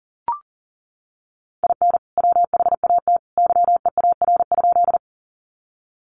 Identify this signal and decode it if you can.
Morse code, heard as a keyed tone, about 40 wpm, 700 Hz. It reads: IN WHAT YEAR?
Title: Sentences from Top 200 Words 40wpm